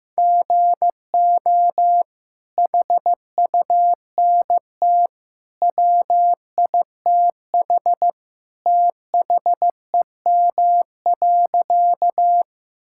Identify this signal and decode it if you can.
Morse code, heard as a keyed tone, about 15 wpm, 700 Hz. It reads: GO HUNT WITH THEM.
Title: Sentences from Top 1600 Words 15wpm